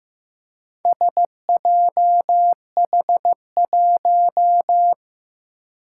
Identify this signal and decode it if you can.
Morse code, heard as a keyed tone, about 15 wpm, 700 Hz. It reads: SJH1
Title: Four Challenging Letter-Number Combinations 15wpm